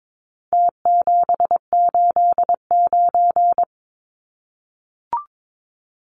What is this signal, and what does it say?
Morse code, heard as a keyed tone, about 22 wpm, 700 Hz. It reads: T789
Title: Numbers - Contest Style - 4-Digit 22wpm